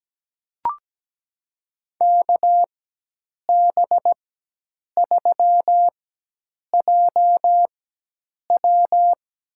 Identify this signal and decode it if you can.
Morse code, heard as a keyed tone, about 17 wpm, spaced as for 8 wpm, 700 Hz. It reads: KB3JW
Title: USA Call Signs - 4x Character Spacing - 17wpm